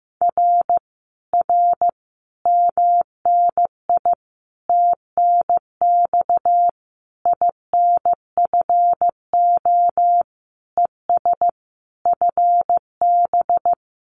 Morse code, heard as a keyed tone, about 15 wpm, 700 Hz. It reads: R R MNI TNX INFO ES FB
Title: Common QSO Phrases Encore 15wpm